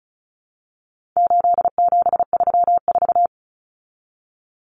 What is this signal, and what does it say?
Morse code, heard as a keyed tone, about 34 wpm, 700 Hz. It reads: 8734
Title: Common Cut Numbers - 4-Digit 35wpm